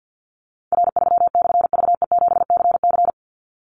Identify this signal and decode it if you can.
Morse code, heard as a keyed tone, about 50 wpm, 700 Hz. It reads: R3/VE7CC
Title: Call Signs - 50wpm